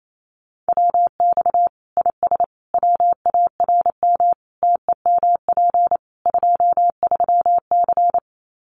Morse code, heard as a keyed tone, about 28 wpm, 700 Hz. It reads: WX IS WARM TEMP 23C
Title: Common QSO Phrases 28wpm